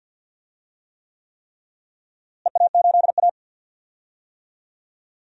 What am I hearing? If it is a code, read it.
Morse code, heard as a keyed tone, about 53 wpm, 700 Hz. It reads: EA8A
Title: DXCC-100 Call Signs - 4 Characters - Repeated 3x - 50wpm